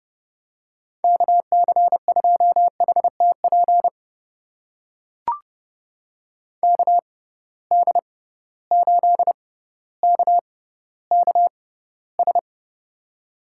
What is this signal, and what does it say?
Morse code, heard as a keyed tone, about 31 wpm, spaced as for 15 wpm, 700 Hz. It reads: KC2HTP KD8KKS
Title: USA Call Signs - 6 Characters - 6x Character Spacing - 30wpm